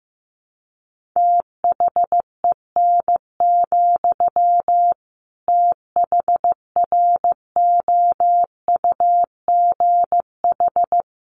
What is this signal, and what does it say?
Morse code, heard as a keyed tone, about 15 wpm, 700 Hz. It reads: THEN, THROUGH
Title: Sentences from Top 1400 Words 15wpm